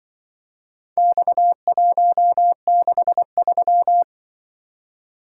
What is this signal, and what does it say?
Morse code, heard as a keyed tone, about 23 wpm, 700 Hz. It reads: X163
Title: Speed-Racing - Four Letter-Number Combinations - 20wpm